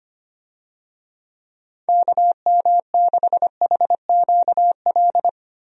Morse code, heard as a keyed tone, about 25 wpm, 700 Hz. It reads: KM6HQL
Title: USA Call Signs - 6 Characters - Repeated 3x - 25wpm